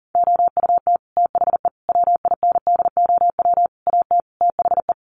Morse code, heard as a keyed tone, about 40 wpm, 700 Hz. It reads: OUT THE WINDOW AT THE
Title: Sets of 5 Words Encore2 40wpm